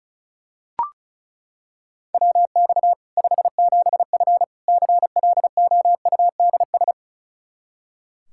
Morse code, heard as a keyed tone, about 36 wpm, 700 Hz. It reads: WX 57F CLOUDS
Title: K1USN SST QSO Practice 35wpm